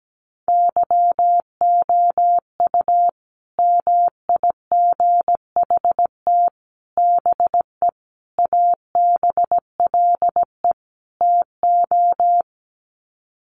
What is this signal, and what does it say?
Morse code, heard as a keyed tone, about 17 wpm, 700 Hz. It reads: YOU MIGHT BE ABLE TO
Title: Sets of 5 Words Encore2 17wpm